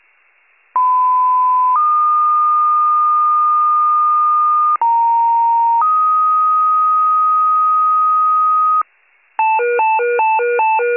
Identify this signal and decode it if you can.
Two-tone page, heard as a one-second tone, then a three-second tone, about 1010 then 1220 Hz, about 930 then 1250 Hz. It is heard from 0.8 to 8.8 s.